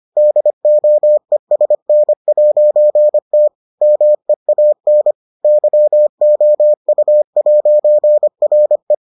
Morse code, heard as a keyed tone, about 25 wpm, 600 Hz. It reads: DOESN'T MEAN YOU'RE